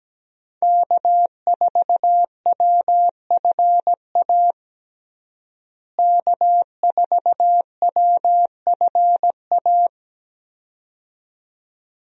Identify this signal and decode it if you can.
Morse code, heard as a keyed tone, about 17 wpm, 700 Hz. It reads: K4WFA K4WFA
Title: USA Call Signs - 5 Characters - Repeated 2x - 17wpm